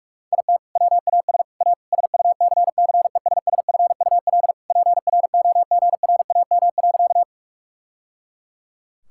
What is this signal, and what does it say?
Morse code, heard as a keyed tone, about 41 wpm, 700 Hz. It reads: IT WAS A SUCCESSFUL PROGRAM.
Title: Sentences from Top 1000 Words - Review Part 5 - 45wpm